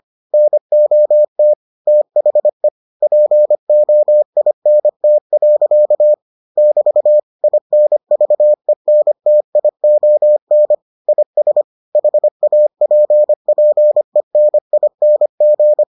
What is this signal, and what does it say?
Morse code, heard as a keyed tone, about 25 wpm, 600 Hz. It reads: NOT THE POINT. <BT> INVENTION IS HAPPENING